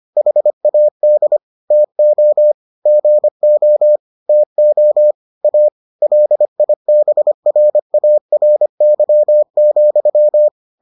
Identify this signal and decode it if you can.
Morse code, heard as a keyed tone, about 25 wpm, 600 Hz. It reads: HAD TO GO TO A LIBRARY,